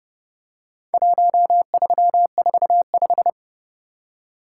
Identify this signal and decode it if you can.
Morse code, heard as a keyed tone, about 26 wpm, 700 Hz. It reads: 1345